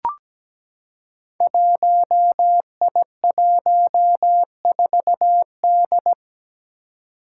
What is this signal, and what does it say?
Morse code, heard as a keyed tone, about 17 wpm, 700 Hz. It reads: 1I14D
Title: Five Challenging Letter-Number Combinations 17wpm